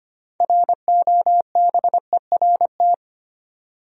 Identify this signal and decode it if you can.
Morse code, heard as a keyed tone, about 25 wpm, 700 Hz. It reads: ROBERT
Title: USA Common Names - 25wpm